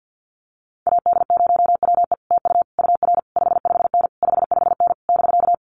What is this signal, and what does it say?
Morse code, heard as a keyed tone, about 48 wpm, 700 Hz. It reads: AD0WE TU UR 55N 55N <BK>
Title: Common QSO Phrases Encore 50wpm